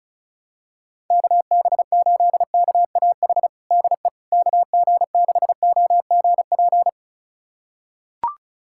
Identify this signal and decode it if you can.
Morse code, heard as a keyed tone, about 36 wpm, 700 Hz. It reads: KB8KAH DE KG6OGP
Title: USA Callsign DE Callsign - 6 Characters - 35wpm